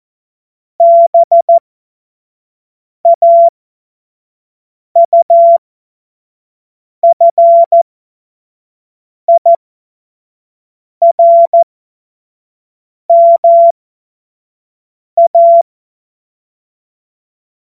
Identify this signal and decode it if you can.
Morse code, heard as a keyed tone, about 14 wpm, spaced as for 5 wpm, 680 Hz. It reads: BAUFIRMA
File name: Morsekurs / Witze / Witz 05 Dezember 2023 25 F70